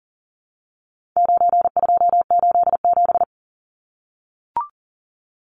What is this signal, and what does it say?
Morse code, heard as a keyed tone, about 40 wpm, 700 Hz. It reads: 9287
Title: Numbers - 4-Digit 40wpm